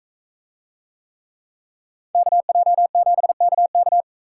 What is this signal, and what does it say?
Morse code, heard as a keyed tone, about 42 wpm, 700 Hz. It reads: KJ7KK